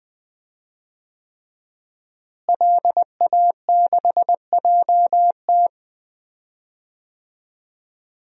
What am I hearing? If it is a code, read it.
Morse code, heard as a keyed tone, about 20 wpm, 700 Hz. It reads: LA6JT
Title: DXCC-100 Call Signs - 5 Characters - Repeated 3x - 20wpm